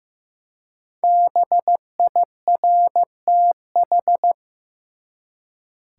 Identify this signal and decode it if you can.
Morse code, heard as a keyed tone, about 15 wpm, 700 Hz. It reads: BIRTH
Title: Five Letter Words - 15wpm